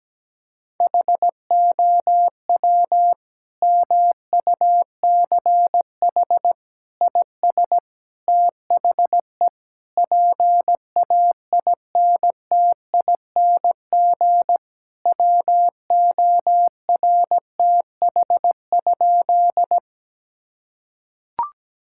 Morse code, heard as a keyed tone, about 17 wpm, 700 Hz. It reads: HOW MUCH IS THE PAINTING WORTH?
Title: Sentences from Top 1000 Words - Review Part 6 - 17wpm